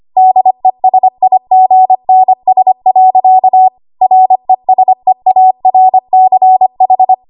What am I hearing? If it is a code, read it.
Morse code, heard as a keyed tone, about 25 wpm, 750 Hz. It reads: DESIGNS. RESEARCH
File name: Archive / 25 WPM / 25WPM